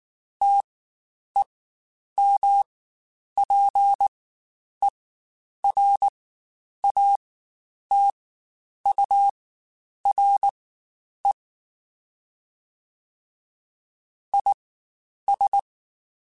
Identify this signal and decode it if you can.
Morse code, heard as a keyed tone, about 19 wpm, spaced as for 9 wpm, 800 Hz. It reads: TEMPERATURE IS
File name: Spaced Characters / Spaced Characters-mp3s / Weather 5 1 sp @18